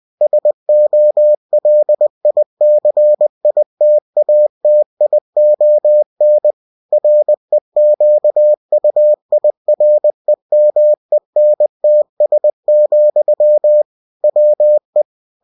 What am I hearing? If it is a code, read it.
Morse code, heard as a keyed tone, about 20 wpm, 600 Hz. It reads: SOLICITATION REQUIREMENTS, WE